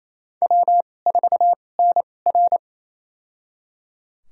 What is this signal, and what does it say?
Morse code, heard as a keyed tone, about 27 wpm, spaced as for 20 wpm, 700 Hz. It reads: W4NR
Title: USA Call Signs - 4 Characters - 2x Character Spacing - 28wpm